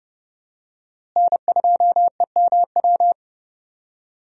Morse code, heard as a keyed tone, about 31 wpm, 700 Hz. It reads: N2EMW
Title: USA Call Signs - 5 Characters - Repeated 2x - 30wpm